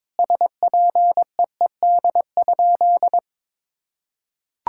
Morse code, heard as a keyed tone, about 21 wpm, 700 Hz. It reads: SPEED?
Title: Sentences from Top 1000 Words - Review Part 5 - 22wpm